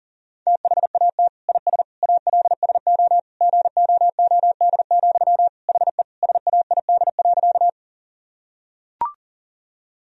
Morse code, heard as a keyed tone, about 40 wpm, 700 Hz. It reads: THAT IS ALSO GOOD, HE SAID.
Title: Sentences from Top 1000 Words - Review Part 1 - 40wpm